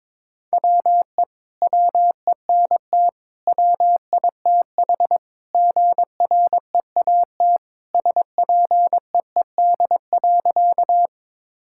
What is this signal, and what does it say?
Morse code, heard as a keyed tone, about 22 wpm, 700 Hz. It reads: WE WENT WITH GREAT SPEED.
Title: Sentences from Top 1000 Words - Review Part 5 - 22wpm